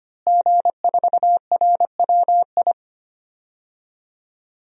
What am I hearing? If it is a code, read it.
Morse code, heard as a keyed tone, about 25 wpm, 700 Hz. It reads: G4RWI
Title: DXCC-100 Call Signs - 5 Characters - Repeated 2x - 25wpm